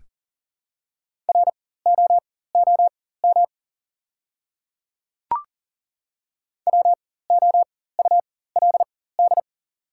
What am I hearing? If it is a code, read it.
Morse code, heard as a keyed tone, about 37 wpm, spaced as for 18 wpm, 690 Hz. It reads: ROOM WOULD
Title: 4x Character Spacing - Top 300 Words - 40wpm